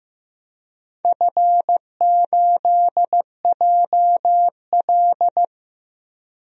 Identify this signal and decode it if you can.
Morse code, heard as a keyed tone, about 15 wpm, 700 Hz. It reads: F8JL
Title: DXCC-100 Call Signs - 4 Characters - 15wpm